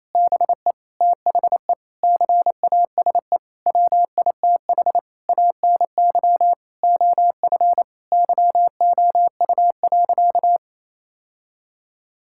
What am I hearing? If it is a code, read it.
Morse code, heard as a keyed tone, about 27 wpm, 700 Hz. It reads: BE THE CASE WITH ANY OF YOU.
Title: Sentences from Top 1600 Words 28wpm